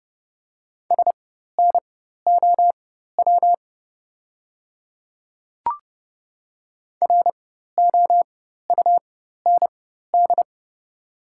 Morse code, heard as a keyed tone, about 29 wpm, spaced as for 14 wpm, 700 Hz. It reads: SNOW ROUND
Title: 4x Character Spacing - Top 500 Words - 30wpm